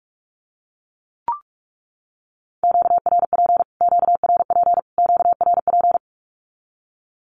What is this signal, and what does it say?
Morse code, heard as a keyed tone, about 44 wpm, 700 Hz. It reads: QRP QRP QRP
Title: Q-Codes Repeated 3x 45wpm